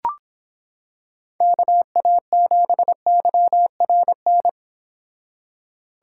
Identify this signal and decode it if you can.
Morse code, heard as a keyed tone, about 26 wpm, 700 Hz. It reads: KA7YRN